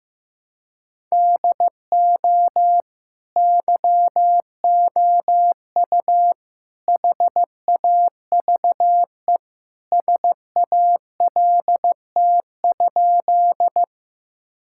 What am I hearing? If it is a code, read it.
Morse code, heard as a keyed tone, about 15 wpm, 700 Hz. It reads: DO YOU HAVE SALT?
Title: Sentences from Top 1000 Words - Review Part 5 - 15wpm